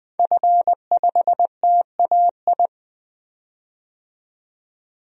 Morse code, heard as a keyed tone, about 20 wpm, 700 Hz. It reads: F5TAI
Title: DXCC-100 Call Signs - 5 Characters - 20wpm